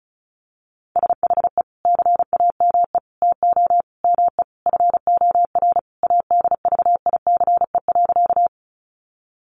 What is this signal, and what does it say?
Morse code, heard as a keyed tone, about 34 wpm, 700 Hz. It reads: SHE CAME TO ME FOR ADVICE.
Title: Sentences from Top 1100 Words 35wpm